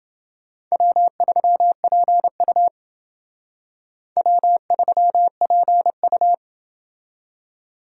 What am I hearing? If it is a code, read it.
Morse code, heard as a keyed tone, about 28 wpm, 700 Hz. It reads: W3PU W3PU